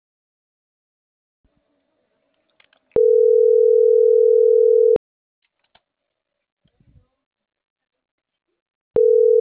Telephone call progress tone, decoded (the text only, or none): ringback tone